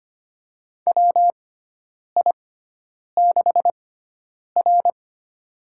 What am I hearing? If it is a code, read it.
Morse code, heard as a keyed tone, about 25 wpm, spaced as for 9 wpm, 700 Hz. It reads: WI6R
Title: USA Call Signs - 4 Characters - 6x Character Spacing - 25wpm